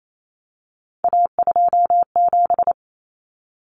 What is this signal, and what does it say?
Morse code, heard as a keyed tone, about 28 wpm, 700 Hz. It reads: A27
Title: Common Cut Numbers - 3-Digit 28wpm